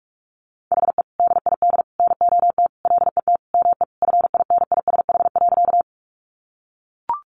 Morse code, heard as a keyed tone, about 44 wpm, 700 Hz. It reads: HE DID NOT LET ME FINISH.
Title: Sentences from Top 1200 Words 45wpm